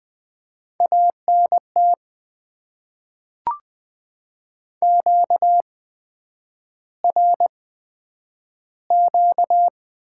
Morse code, heard as a keyed tone, about 20 wpm, spaced as for 9 wpm, 700 Hz. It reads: ANT QRQ
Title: Top 90 QSO Elements - 8x Character Spacing - 20wpm